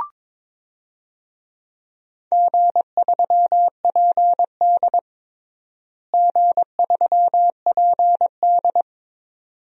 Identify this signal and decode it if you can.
Morse code, heard as a keyed tone, about 22 wpm, 700 Hz. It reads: G3PD G3PD
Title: DXCC-100 Call Signs - 4 Characters - Repeated 2x - 22wpm